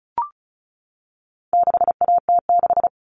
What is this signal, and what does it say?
Morse code, heard as a keyed tone, about 34 wpm, 700 Hz. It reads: 6AT6
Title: All Cut Numbers - 4-Digit 35wpm